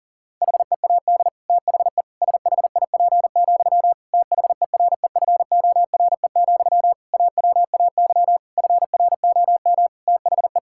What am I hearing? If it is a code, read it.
Morse code, heard as a keyed tone, about 40 wpm, 700 Hz. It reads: HEAD THE SHIP, THEREFORE, AWAY FROM THE
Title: Sentences from Top 1000 Words - Review Part 5 - 40wpm